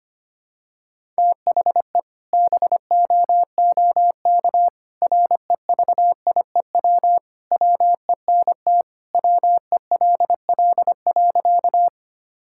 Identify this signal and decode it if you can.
Morse code, heard as a keyed tone, about 25 wpm, 700 Hz. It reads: THE BOOK REVIEW WENT WELL.
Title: Sentences from Top 1000 Words - Review Part 6 - 25wpm